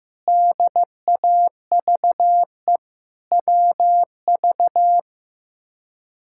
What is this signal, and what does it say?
Morse code, heard as a keyed tone, about 15 wpm, 700 Hz. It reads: DAVE WV
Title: K1USN SST QSO Practice - 15wpm